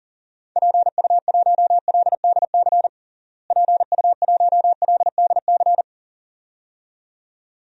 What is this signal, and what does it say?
Morse code, heard as a keyed tone, about 40 wpm, 700 Hz. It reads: PU1LDC PU1LDC
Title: DXCC-100 Call Signs - 6 Characters - Repeated 2x - 40wpm